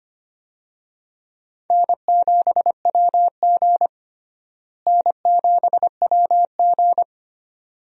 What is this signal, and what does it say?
Morse code, heard as a keyed tone, about 25 wpm, 700 Hz. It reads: N7WG N7WG
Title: USA Call Signs - 4 Characters - Repeated 2x - 25wpm